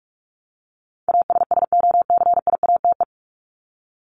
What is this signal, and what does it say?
Morse code, heard as a keyed tone, about 46 wpm, 700 Hz. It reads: ASSOCIATE
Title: Top 1751-2000 Words 45wpm